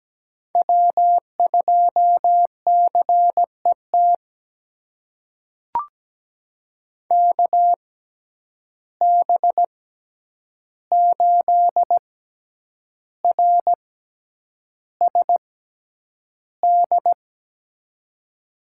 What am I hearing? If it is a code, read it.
Morse code, heard as a keyed tone, about 17 wpm, spaced as for 8 wpm, 700 Hz. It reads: W2CET KB8RSD
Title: USA Call Signs - 6x Character Spacing - 17wpm